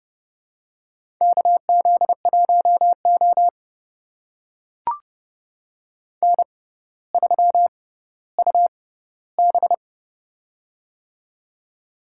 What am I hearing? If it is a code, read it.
Morse code, heard as a keyed tone, about 30 wpm, spaced as for 15 wpm, 700 Hz. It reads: KZ1O N3UB
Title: USA Call Signs - 4 Characters - 6x Character Spacing - 30wpm